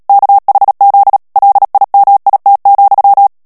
Morse code, heard as a keyed tone, about 37 wpm, 800 Hz. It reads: KHZ LIMIT,
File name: morse mp3 / ANGLAIS texte clair 3 vitesse 35 mots minute